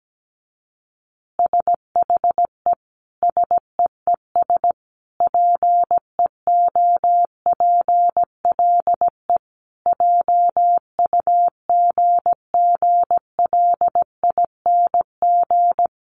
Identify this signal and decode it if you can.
Morse code, heard as a keyed tone, about 17 wpm, 700 Hz. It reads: SHE SEES PEOPLE JUGGLING
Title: Easy Phrases 17wpm